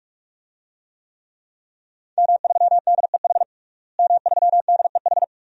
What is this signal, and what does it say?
Morse code, heard as a keyed tone, about 43 wpm, 700 Hz. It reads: M3DEH M3DEH
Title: DXCC-100 Call Signs - 5 Characters - Repeated 2x - 45wpm